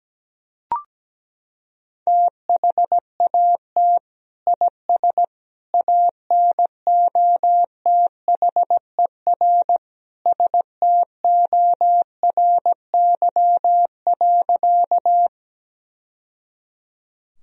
Morse code, heard as a keyed tone, about 17 wpm, 700 Hz. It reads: THAT IS ANOTHER STORY.
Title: Sentences from Top 1000 Words - Review Part 1 - 17wpm